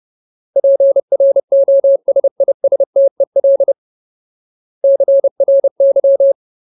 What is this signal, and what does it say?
Morse code, heard as a keyed tone, about 30 wpm, 550 Hz. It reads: PROSISTEL CRY
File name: Ultimate Word List / 30WPM Ultimate